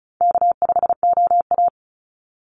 Morse code, heard as a keyed tone, about 34 wpm, 700 Hz. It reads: K5OA
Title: Call Signs - 35wpm